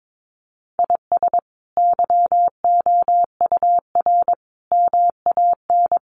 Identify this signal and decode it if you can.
Morse code, heard as a keyed tone, about 22 wpm, 700 Hz. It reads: IS YOUR MAN
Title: Sentences from Top 400 Words 22wpm